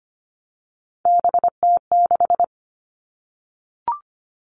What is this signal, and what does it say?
Morse code, heard as a keyed tone, about 25 wpm, 700 Hz. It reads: BT6
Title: All Cut Numbers - 3-Digit 25wpm